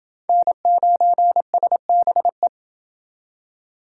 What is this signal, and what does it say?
Morse code, heard as a keyed tone, about 28 wpm, 700 Hz. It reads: N9SBE